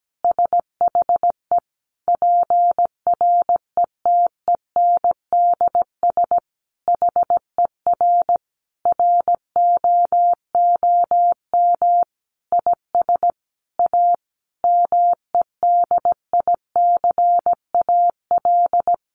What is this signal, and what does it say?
Morse code, heard as a keyed tone, about 17 wpm, 700 Hz. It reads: SHE PRETENDS HER ROOM IS A MEDICAL
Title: Additional Easy Phrases 17wpm